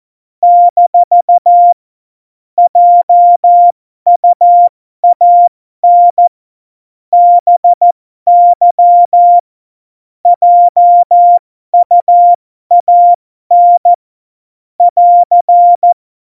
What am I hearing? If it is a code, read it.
Morse code, heard as a keyed tone, about 14 wpm, spaced as for 12 wpm, 700 Hz. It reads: - JUAN BY JUAN <AR>
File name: Morsekurs / Witze